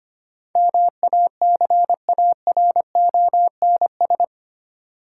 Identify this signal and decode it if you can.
Morse code, heard as a keyed tone, about 25 wpm, 700 Hz. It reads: MACARONS
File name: practice / 25wpm / Difficult Words with Hints - 25wpm